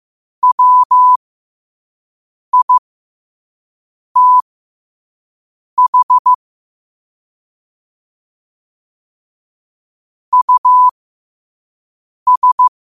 Morse code, heard as a keyed tone, about 15 wpm, spaced as for 5 wpm, 1000 Hz. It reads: WITH US
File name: qotdfiles / 5wpm